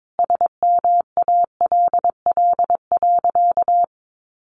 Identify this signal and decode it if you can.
Morse code, heard as a keyed tone, about 22 wpm, 700 Hz. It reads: SMALL.
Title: Additional Easy Phrases 22wpm